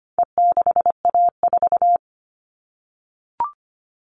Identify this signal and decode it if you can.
Morse code, heard as a keyed tone, about 25 wpm, 700 Hz. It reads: E6A4
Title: All Cut Numbers - 4-Digit 25wpm